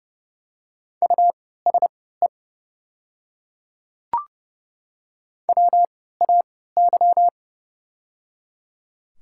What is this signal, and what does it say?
Morse code, heard as a keyed tone, about 32 wpm, spaced as for 17 wpm, 700 Hz. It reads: USE WAY
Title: 3x Character Spacing - Top 100 Words - 30wpm